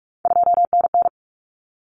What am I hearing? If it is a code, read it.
Morse code, heard as a keyed tone, about 44 wpm, 700 Hz. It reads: 2NN
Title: Common Cut Numbers - 3-Digit 45wpm